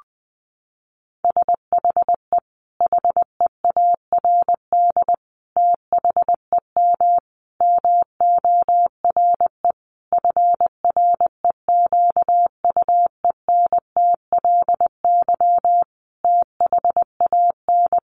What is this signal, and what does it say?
Morse code, heard as a keyed tone, about 19 wpm, 700 Hz. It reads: SHE HEARD THEM MORE FREQUENTLY THAN
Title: Sentences from Top 1300 Words 20wpm